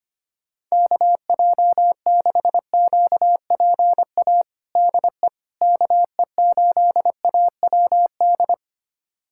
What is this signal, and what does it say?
Morse code, heard as a keyed tone, about 25 wpm, 700 Hz. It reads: KJ6QPA DE KE8AWD
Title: USA Callsign DE Callsign - 6 Characters - 25wpm